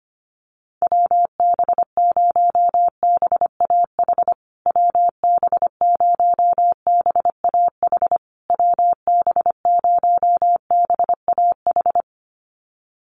Morse code, heard as a keyed tone, about 25 wpm, 700 Hz. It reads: WB0BAH WB0BAH WB0BAH
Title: USA Call Signs Repeated 3x 25wpm